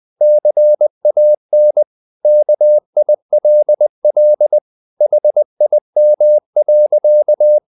Morse code, heard as a keyed tone, about 20 wpm, 600 Hz. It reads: CAN KILL HIM.
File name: the old man and the sea / the old man and the sea part 1 20wpm